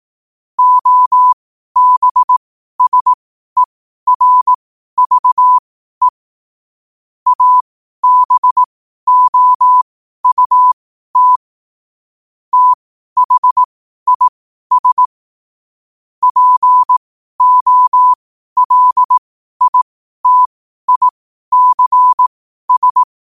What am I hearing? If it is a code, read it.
Morse code, heard as a keyed tone, about 18 wpm, spaced as for 12 wpm, 1000 Hz. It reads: OBSERVE ABOUT THIS POLITICS